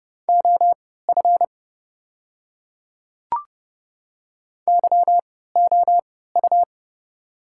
Morse code, heard as a keyed tone, about 30 wpm, spaced as for 17 wpm, 700 Hz. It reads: OF YOU
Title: 3x Character Spacing - Top 100 Words - 30wpm